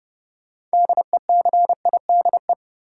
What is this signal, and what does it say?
Morse code, heard as a keyed tone, about 29 wpm, 700 Hz. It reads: DECIDE
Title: Six Letter Words - 30wpm